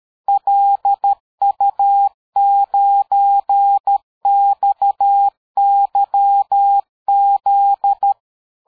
Morse code, heard as a keyed tone, about 13 wpm, 800 Hz. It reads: LU9XYZ
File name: dk5ke / QSO 60BpM